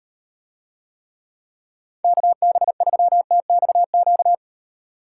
Morse code, heard as a keyed tone, about 36 wpm, 700 Hz. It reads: KB3TXQ